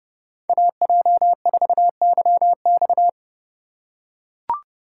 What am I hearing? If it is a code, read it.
Morse code, heard as a keyed tone, about 29 wpm, 700 Hz. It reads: AJ4YX